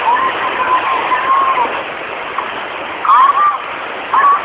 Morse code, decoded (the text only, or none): none